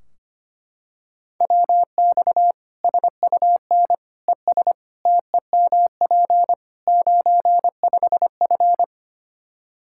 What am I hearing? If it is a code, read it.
Morse code, heard as a keyed tone, about 25 wpm, 700 Hz. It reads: WX SUN ES TEMP 95F
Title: Practice QSO - Exchange 2 - Info - 25wpm